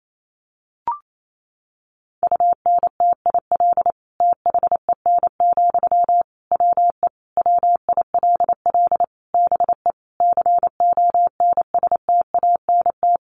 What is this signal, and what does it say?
Morse code, heard as a keyed tone, about 28 wpm, 700 Hz. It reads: UNTIL THEN, WE WILL BE CONSTANT
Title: Sentences from Top 1500 Words 28wpm